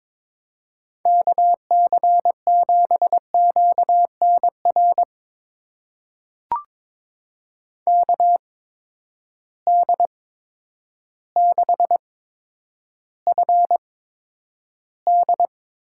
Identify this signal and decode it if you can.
Morse code, heard as a keyed tone, about 21 wpm, spaced as for 10 wpm, 700 Hz. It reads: KC7QNR KD6FD
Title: USA Call Signs - 6 Characters - 8x Character Spacing - 22wpm